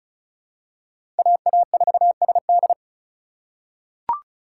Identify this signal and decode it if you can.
Morse code, heard as a keyed tone, about 34 wpm, 700 Hz. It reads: AA4SD
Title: USA Call Signs - 5 Characters - 4x Character Spacing - 35wpm